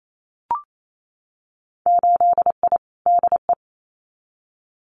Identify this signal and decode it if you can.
Morse code, heard as a keyed tone, about 28 wpm, 700 Hz. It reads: 8I DE
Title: Winter Field Day Exchanges Encore 28wpm